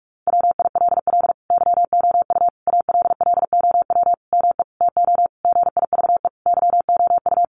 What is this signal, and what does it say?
Morse code, heard as a keyed tone, about 44 wpm, 700 Hz. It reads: WILL YOU ALLOW ME TO GIVE YOU
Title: Sentences from Top 1200 Words 45wpm